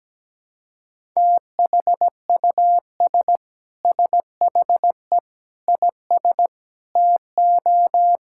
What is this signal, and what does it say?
Morse code, heard as a keyed tone, about 17 wpm, 700 Hz. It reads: THUS SHE IS TO
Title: Sentences from Top 1000 Words - Review Part 5 - 17wpm